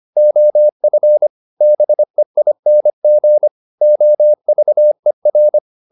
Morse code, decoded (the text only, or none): OF BEING OVER